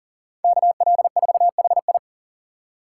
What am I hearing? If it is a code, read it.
Morse code, heard as a keyed tone, about 40 wpm, 700 Hz. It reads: KL4HI
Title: USA Call Signs - 5 Characters - 2x Character Spacing - 40wpm